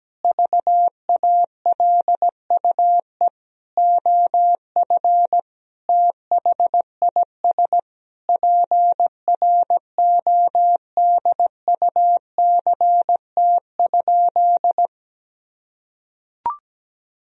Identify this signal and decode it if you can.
Morse code, heard as a keyed tone, about 17 wpm, 700 Hz. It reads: VALUE OF THIS PRODUCT?
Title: Sentences from Top 1000 Words - Review Part 3 - 17wpm